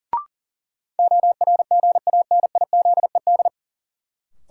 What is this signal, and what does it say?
Morse code, heard as a keyed tone, about 40 wpm, 700 Hz. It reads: ORGANIZED
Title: Sentences from Top 1300 Words - New Word Drill - 40wpm